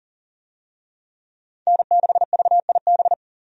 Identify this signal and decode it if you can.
Morse code, heard as a keyed tone, about 39 wpm, 700 Hz. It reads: N6VIB
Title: USA Call Signs - 5 Characters - 40wpm